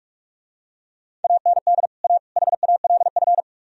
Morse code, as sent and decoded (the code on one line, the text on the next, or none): .- -. -.. / .- / .... .- .-.. ..-.
AND A HALF